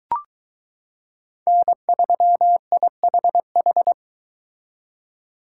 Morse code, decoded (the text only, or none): N3IHH